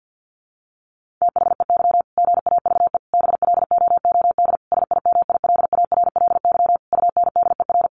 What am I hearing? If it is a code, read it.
Morse code, heard as a keyed tone, about 50 wpm, 700 Hz. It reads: THEY GAVE BLOOD SIMILARLY UNDER
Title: Sentences from Top 1400 Words 50wpm